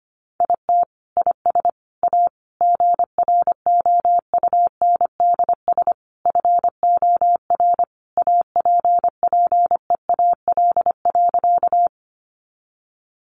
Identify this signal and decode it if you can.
Morse code, heard as a keyed tone, about 25 wpm, 700 Hz. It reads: IT IS A GROUNDS FOR APPEAL.
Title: Sentences from Top 1300 Words 25wpm